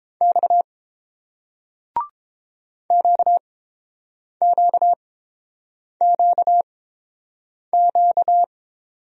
Morse code, heard as a keyed tone, about 30 wpm, spaced as for 7 wpm, 700 Hz. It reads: X QQQQ